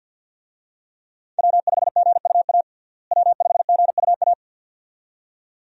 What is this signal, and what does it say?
Morse code, heard as a keyed tone, about 50 wpm, 700 Hz. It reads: W5GUA W5GUA
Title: USA Call Signs - 5 Characters - Repeated 2x - 50wpm